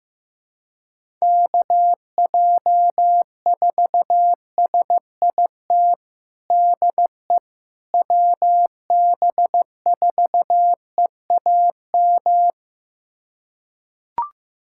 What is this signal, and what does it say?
Morse code, heard as a keyed tone, about 15 wpm, 700 Hz. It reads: KJ4SIT DE WB4EAM
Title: USA Callsign DE Callsign - 6 Characters - 15wpm